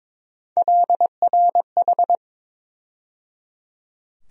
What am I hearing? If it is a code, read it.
Morse code, heard as a keyed tone, about 21 wpm, 700 Hz. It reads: LRH
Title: USA Call Sign Postfixes - 3 Letters 22wpm